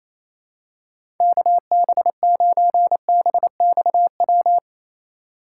Morse code, heard as a keyed tone, about 28 wpm, 700 Hz. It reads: KB9BXW
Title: USA Call Signs - 6 Characters - 4x Character Spacing - 28wpm